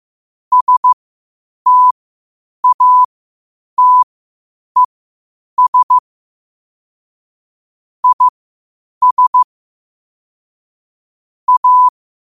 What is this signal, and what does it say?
Morse code, heard as a keyed tone, about 15 wpm, spaced as for 8 wpm, 1000 Hz. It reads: STATES IS A
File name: qotdfiles / 8wpm